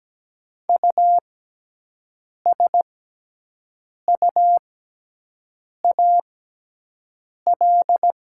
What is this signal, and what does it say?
Morse code, heard as a keyed tone, about 17 wpm, spaced as for 6 wpm, 700 Hz. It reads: USUAL
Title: 6x Character Spacing - Top 500 Words - 17wpm